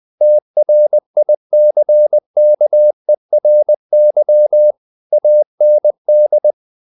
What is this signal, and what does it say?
Morse code, heard as a keyed tone, about 20 wpm, 600 Hz. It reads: TRICKERY AND
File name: the old man and the sea part 2 20wpm